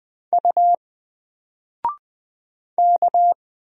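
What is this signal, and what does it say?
Morse code, heard as a keyed tone, about 20 wpm, 700 Hz. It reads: U K
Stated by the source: Single letter and number Morse drill at 20 wpm, from N4LCD.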